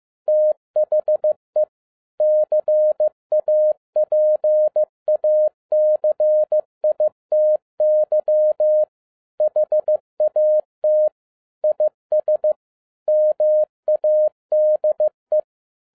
Morse code, heard as a keyed tone, about 15 wpm, 610 Hz. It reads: THE CAPACITY HAT IS MADE
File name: 2012-01-mp3 / n07-ARA JR0DIL-15wpm